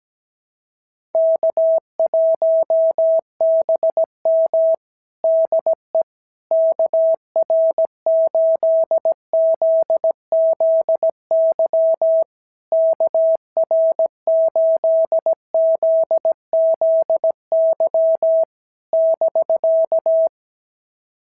Morse code, heard as a keyed tone, about 16 wpm, 650 Hz. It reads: K1BM DE KR8ZZY KR8ZZY <BK>
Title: SOTA QSO Practice 17wpm